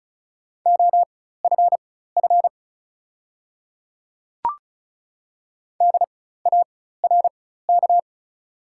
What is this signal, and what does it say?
Morse code, heard as a keyed tone, about 34 wpm, spaced as for 16 wpm, 690 Hz. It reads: OFF DARK
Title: 4x Character Spacing - Top 500 Words - 35wpm